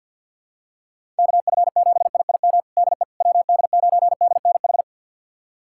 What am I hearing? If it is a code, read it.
Morse code, heard as a keyed tone, about 52 wpm, 700 Hz. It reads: KF7IIM DE WD9DNH
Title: USA Callsign DE Callsign - 6 Characters - 50wpm